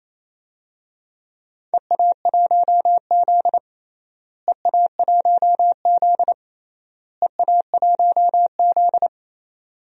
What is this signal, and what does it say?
Morse code, heard as a keyed tone, about 27 wpm, 700 Hz. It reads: EA1Z EA1Z EA1Z
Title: DXCC-100 Call Signs - 4 Characters - Repeated 3x - 28wpm